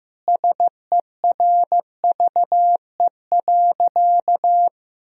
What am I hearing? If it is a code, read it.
Morse code, heard as a keyed tone, about 15 wpm, 700 Hz. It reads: SERVE.
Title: Sentences from Top 1000 Words - Review Part 4 - 15wpm